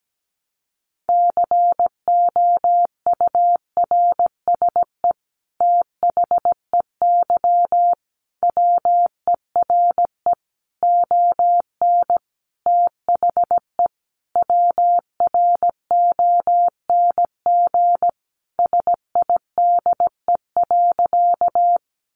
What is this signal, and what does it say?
Morse code, heard as a keyed tone, about 17 wpm, 700 Hz. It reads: COURSE THEY WERE ON THE WRONG SIDE.
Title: Sentences from Top 900 Words 17wpm